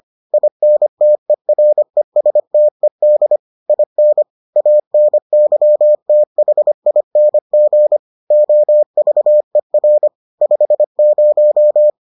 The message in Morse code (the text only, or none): INTERESTED IN ANYTHING OVER 50